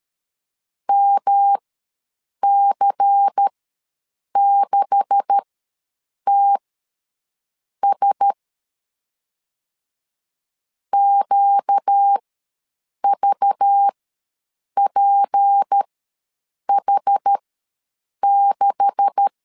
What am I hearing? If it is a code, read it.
Morse code, heard as a keyed tone, about 13 wpm, spaced as for 7 wpm, 800 Hz. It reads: MC6TS QVPH6
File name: LEZIONI / 9 286 ws 12wpm 3351